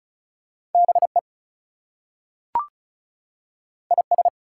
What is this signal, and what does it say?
Morse code, heard as a keyed tone, about 36 wpm, 700 Hz. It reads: BE IS
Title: Two Letter Words - 35wpm